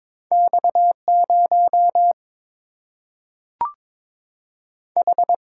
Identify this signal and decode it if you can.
Morse code, heard as a keyed tone, about 22 wpm, 700 Hz. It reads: X 0 H
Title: Single-Letter Single-Number 22wpm